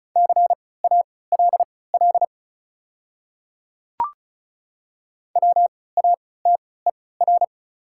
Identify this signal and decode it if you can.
Morse code, heard as a keyed tone, about 36 wpm, spaced as for 20 wpm, 690 Hz. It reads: CALL WATER
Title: 3x Character Spacing - Top 100 Words - 35wpm